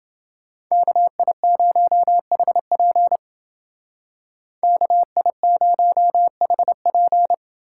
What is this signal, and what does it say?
Morse code, heard as a keyed tone, about 28 wpm, 700 Hz. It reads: KI0HP KI0HP